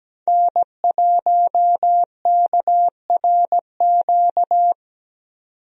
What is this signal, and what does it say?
Morse code, heard as a keyed tone, about 17 wpm, 700 Hz. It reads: N1KRQ
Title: USA Call Signs - 5 Characters - 17wpm